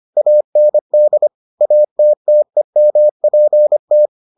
Morse code, heard as a keyed tone, about 25 wpm, 600 Hz. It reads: AND ATTEMPT